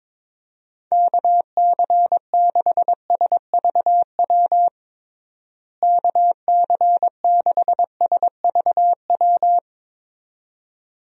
Morse code, heard as a keyed tone, about 22 wpm, 700 Hz. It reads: KC6SVW KC6SVW
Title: USA Call Signs - 6 Characters - Repeated 2x - 22wpm